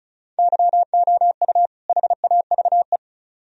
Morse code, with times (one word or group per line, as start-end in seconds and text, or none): 0.4-1.7 YOU
1.9-3.0 HAVE